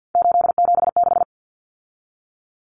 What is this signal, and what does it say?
Morse code, heard as a keyed tone, about 53 wpm, 700 Hz. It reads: 876
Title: Common Cut Numbers - 3-Digit 50wpm